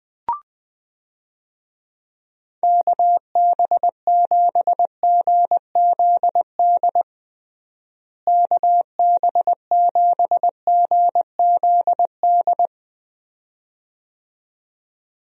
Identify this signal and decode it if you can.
Morse code, heard as a keyed tone, about 20 wpm, 700 Hz. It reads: KB7GZD KB7GZD
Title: USA Call Signs - 6 Characters - Repeated 2x - 20wpm